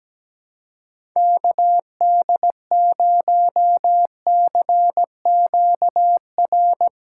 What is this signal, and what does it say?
Morse code, heard as a keyed tone, about 17 wpm, 700 Hz. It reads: KD0CQR
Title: USA Call Signs - 6x Character Spacing - 17wpm